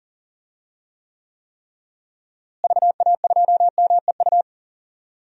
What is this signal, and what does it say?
Morse code, heard as a keyed tone, about 38 wpm, 700 Hz. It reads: VA2MEU
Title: DXCC-100 Call Signs - 6 Characters - Repeated 3x - 40wpm